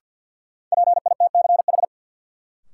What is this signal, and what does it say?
Morse code, heard as a keyed tone, about 53 wpm, 700 Hz. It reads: PITCH
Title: Sentences from Top 1300 Words - New Word Drill Alt - 50wpm